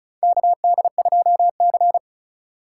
Morse code, heard as a keyed tone, about 34 wpm, 700 Hz. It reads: KD2C
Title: USA Call Signs - 2x Character Spacing - 35wpm